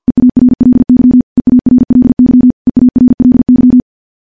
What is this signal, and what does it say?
Morse code, heard as a keyed tone, about 50 wpm, 250 Hz. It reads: EARLY EARLY EARLY
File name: morse / morse-training-50WPM